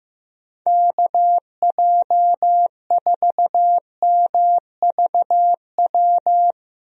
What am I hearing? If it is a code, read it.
Morse code, heard as a keyed tone, about 15 wpm, 700 Hz. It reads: KJ4MVW
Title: USA Call Signs - 6 Characters - Repeated 3x - 15wpm